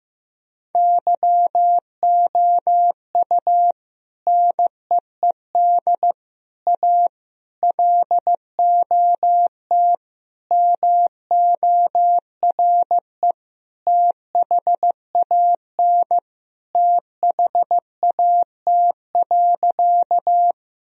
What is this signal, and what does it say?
Morse code, heard as a keyed tone, about 15 wpm, 700 Hz. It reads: YOU NEED A LOT MORE THAN THAT.
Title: Sentences from Top 1000 Words - Review Part 3 - 15wpm